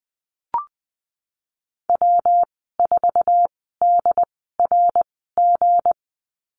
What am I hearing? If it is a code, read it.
Morse code, heard as a keyed tone, about 20 wpm, spaced as for 15 wpm, 700 Hz. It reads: W4DRG
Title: 161 Lesson 26g - Letter g - Call Signs 20wpm